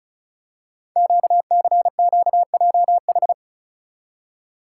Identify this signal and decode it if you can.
Morse code, heard as a keyed tone, about 36 wpm, 700 Hz. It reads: QCQJH
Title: Code Groups - 5 Letters - 35wpm